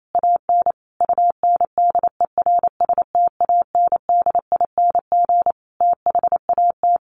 Morse code, heard as a keyed tone, about 28 wpm, 700 Hz. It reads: AN UNDERSTANDING THAT
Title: Sentences from Top 1500 Words 28wpm